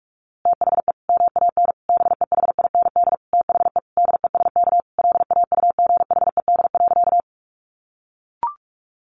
Morse code, heard as a keyed tone, about 44 wpm, 700 Hz. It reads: THE MAN BEHIND THE DESK LAUGHED.
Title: A Sound Of Thunder 45wpm